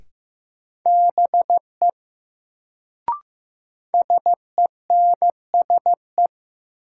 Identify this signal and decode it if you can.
Morse code, heard as a keyed tone, about 15 wpm, 700 Hz. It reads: BE SENSE
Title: Sets of 2 Words - New Word Drill - 15wpm